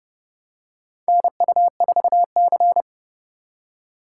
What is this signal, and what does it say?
Morse code, heard as a keyed tone, about 30 wpm, 700 Hz. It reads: NU4C
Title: USA Call Signs - 4 Characters - 4x Character Spacing - 30wpm